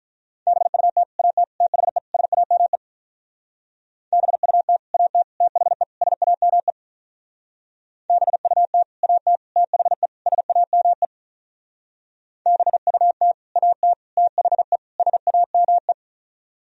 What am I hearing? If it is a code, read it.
Morse code, heard as a keyed tone, about 43 wpm, 700 Hz. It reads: BUT AT THE SAME BUT AT THE SAME BUT AT THE SAME BUT AT THE SAME
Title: Speed-Racing - Sets of 4 Words 35wpm